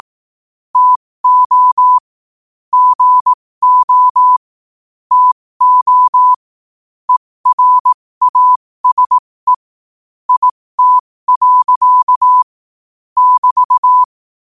Morse code, heard as a keyed tone, about 18 wpm, spaced as for 15 wpm, 1000 Hz. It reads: TO GO TO ERASE IT. <BT>
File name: qotdfiles / 15wpm